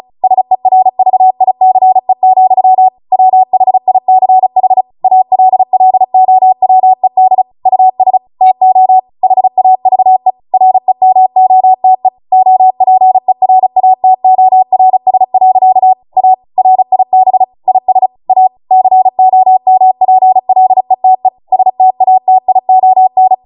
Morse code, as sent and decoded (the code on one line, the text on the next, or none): ... .-. ...- .. -.-. --..-- / .-- .... .. -.-. .... / .- .-.. .-.. --- .-- . -.. / ..- ... / - --- / .... .- ...- . / .-. -- --- - . / --- .--. .-. .- - --- .-. ... .-.-.- / .- / .-. .. -... / .. ... / .- / -.-. --- -- .--. .-.. - . / ... - .- - .. --- -.
SERVICE, WHICH ALLOWED US TO HAVE REMOTE OPERATORS. A RIB IS A COMPLETE STATION